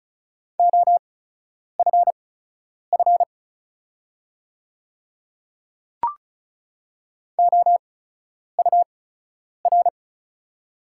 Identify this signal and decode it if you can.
Morse code, heard as a keyed tone, about 34 wpm, spaced as for 10 wpm, 700 Hz. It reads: OFF OUR
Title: 8x Character Spacing - Top 200 Words - 35wpm